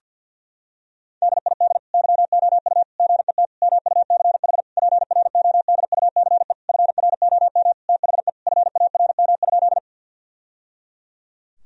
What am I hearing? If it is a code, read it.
Morse code, heard as a keyed tone, about 50 wpm, 700 Hz. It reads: DID YOU GET MUCH PRODUCE FROM THE FARM?
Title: Sentences from Top 1000 Words - Review Part 3 - 50wpm